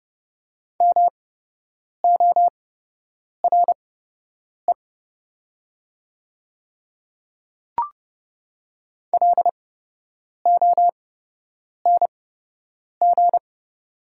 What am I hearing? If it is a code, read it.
Morse code, heard as a keyed tone, about 30 wpm, spaced as for 8 wpm, 700 Hz. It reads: MORE LONG
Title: 8x Character Spacing - Top 100 Words - 30wpm